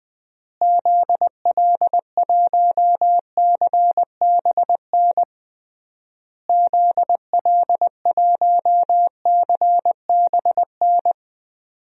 Morse code, heard as a keyed tone, about 20 wpm, 700 Hz. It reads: ZL1CBN ZL1CBN
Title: DXCC-100 Call Signs - 6 Characters - Repeated 3x - 20wpm